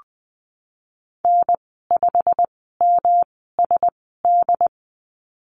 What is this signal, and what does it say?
Morse code, heard as a keyed tone, about 20 wpm, spaced as for 14 wpm, 700 Hz. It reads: N5MSD
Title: 128 Lesson 21g - Number 6 - Call Signs 20wpm